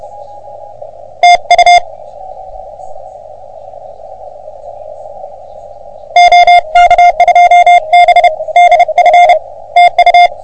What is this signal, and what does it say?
Morse code, heard as a keyed tone, about 30 wpm, spaced as for 27 wpm, 720 Hz, with radio noise. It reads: TU OK2BDF TU
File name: OM3PA PILUP CUC 388